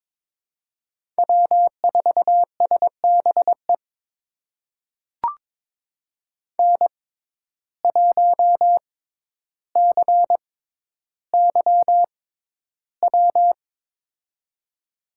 Morse code, heard as a keyed tone, about 22 wpm, spaced as for 11 wpm, 700 Hz. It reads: W4SBE N1CYW
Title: USA Call Signs - 5 Characters - 6x Character Spacing - 22wpm